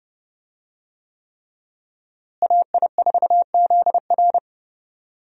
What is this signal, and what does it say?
Morse code, heard as a keyed tone, about 32 wpm, 700 Hz. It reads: AI4ZR
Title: USA Call Signs - 5 Characters - Repeated 3x - 30wpm